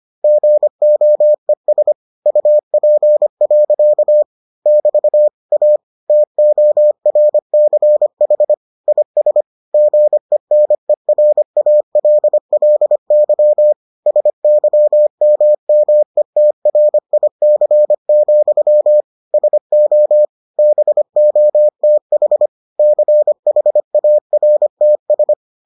Morse code, heard as a keyed tone, about 25 wpm, 600 Hz. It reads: GOES UP. <BT> A TORCH IS GENERALLY SYMMETRIC, SO BOTH CHARTS